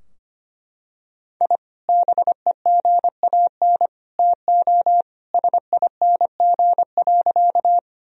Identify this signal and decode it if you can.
Morse code, heard as a keyed tone, about 25 wpm, 700 Hz. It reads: I BEGAN TO SING.
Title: Sentences from Top 1000 Words - Review Part 2 - 25wpm